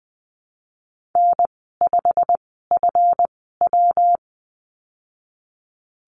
Morse code, heard as a keyed tone, about 20 wpm, spaced as for 15 wpm, 700 Hz. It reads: N5FW
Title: 147 Lesson 24g - Letter y - Call Signs 20wpm